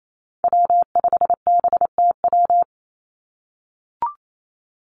Morse code, heard as a keyed tone, about 28 wpm, 700 Hz. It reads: W5BTW
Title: USA Call Signs 28wpm